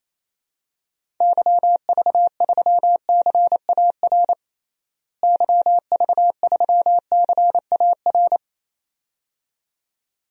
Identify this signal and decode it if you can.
Morse code, heard as a keyed tone, about 28 wpm, 700 Hz. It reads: YV3CAR YV3CAR
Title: DXCC-100 Call Signs - 6 Characters - Repeated 2x - 28wpm